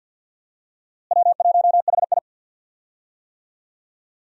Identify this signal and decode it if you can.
Morse code, heard as a keyed tone, about 51 wpm, 700 Hz. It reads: W1HI
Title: USA Call Signs - 4 Characters - Repeated 3x - 50wpm